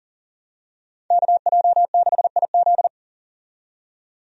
Morse code, heard as a keyed tone, about 40 wpm, 700 Hz. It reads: KJ6IZ
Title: USA Call Signs - 5 Characters - 40wpm